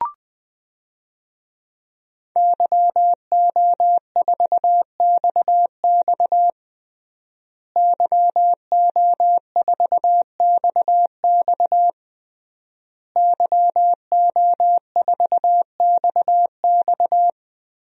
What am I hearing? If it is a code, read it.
Morse code, heard as a keyed tone, about 21 wpm, 700 Hz. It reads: YO4XX YO4XX YO4XX
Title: DXCC-100 Call Signs - 5 Characters - Repeated 3x - 20wpm